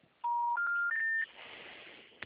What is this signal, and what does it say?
Special information tone.
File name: out- -2025- - - .1731